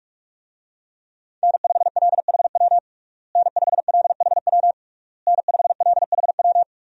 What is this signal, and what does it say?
Morse code, heard as a keyed tone, about 46 wpm, 700 Hz. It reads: N5LHW N5LHW N5LHW
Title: USA Call Signs - 5 Characters - Repeated 3x - 45wpm